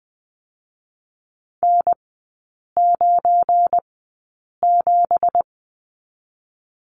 Morse code, heard as a keyed tone, about 20 wpm, 700 Hz. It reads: N 9 7
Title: 184 Lesson 30e - Letter b - Three Characters 20wpm